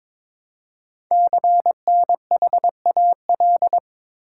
Morse code, heard as a keyed tone, about 22 wpm, 700 Hz. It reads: CNHAL
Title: Code Groups - 5 Letters - 22wpm